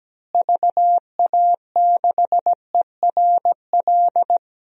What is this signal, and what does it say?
Morse code, heard as a keyed tone, about 17 wpm, 700 Hz. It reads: VA6ERL
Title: DXCC-100 Call Signs - 6 Characters - Repeated 2x - 17wpm